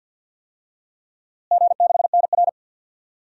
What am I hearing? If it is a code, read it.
Morse code, heard as a keyed tone, about 52 wpm, 700 Hz. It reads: G6NR
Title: DXCC-100 Call Signs - 4 Characters - Repeated 3x - 50wpm